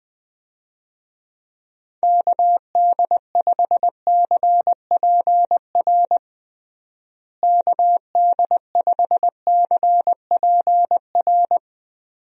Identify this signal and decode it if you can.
Morse code, heard as a keyed tone, about 20 wpm, 700 Hz. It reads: KD5CPR KD5CPR
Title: USA Call Signs - 6 Characters - Repeated 2x - 20wpm